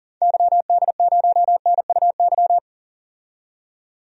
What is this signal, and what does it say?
Morse code, heard as a keyed tone, about 40 wpm, 700 Hz. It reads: YD0NUY
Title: DXCC-100 Call Signs - 6 Characters - 40wpm